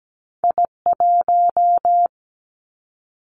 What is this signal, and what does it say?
Morse code, heard as a keyed tone, about 17 wpm, 700 Hz. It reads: I 1
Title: Two Challenging Letter-Number Combinations 17wpm